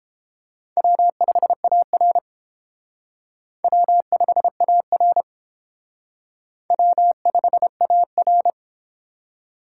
Morse code, heard as a keyed tone, about 30 wpm, 700 Hz. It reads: W5AR W5AR W5AR